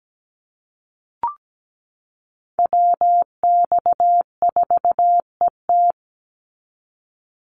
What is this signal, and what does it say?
Morse code, heard as a keyed tone, about 17 wpm, 700 Hz. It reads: WX4ET
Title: NA SOTA Chaser Callsigns 17wpm